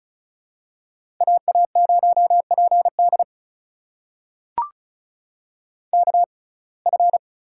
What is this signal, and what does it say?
Morse code, heard as a keyed tone, about 36 wpm, spaced as for 25 wpm, 700 Hz. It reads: AA0PD KF
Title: USA Call Signs - 5 Characters - 6x Character Spacing - 35wpm